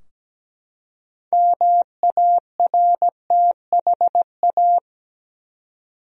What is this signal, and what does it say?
Morse code, heard as a keyed tone, about 17 wpm, 700 Hz. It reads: MARTHA
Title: USA Common Names - 17wpm